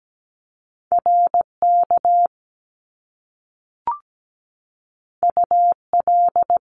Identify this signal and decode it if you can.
Morse code, heard as a keyed tone, about 17 wpm, 700 Hz. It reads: RK UL
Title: Top Two Letter Combinations in Words 17wpm